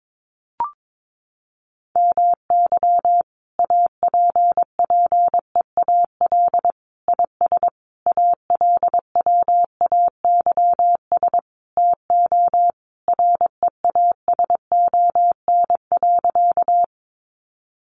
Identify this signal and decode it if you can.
Morse code, heard as a keyed tone, about 22 wpm, 700 Hz. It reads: MY APPEAL IS ALWAYS TO REASON.
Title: Sentences from Top 1300 Words 22wpm